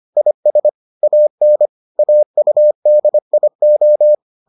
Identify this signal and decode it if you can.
Morse code, heard as a keyed tone, about 25 wpm, 600 Hz. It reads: IS AN AUDIO